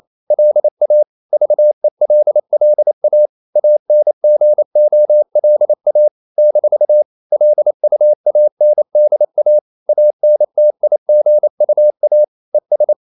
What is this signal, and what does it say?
Morse code, heard as a keyed tone, about 28 wpm, 600 Hz. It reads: LA VELLA ANGOLA - LUANDA ANTIGUA ES
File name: CW / Practice / NationalCapitals28